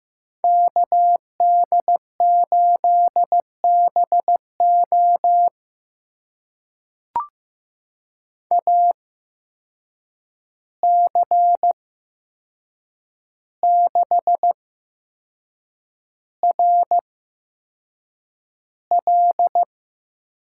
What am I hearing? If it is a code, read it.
Morse code, heard as a keyed tone, about 15 wpm, spaced as for 6 wpm, 700 Hz. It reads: KD8BO AC6RL
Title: USA Call Signs - 5 Characters - 8x Character Spacing - 15wpm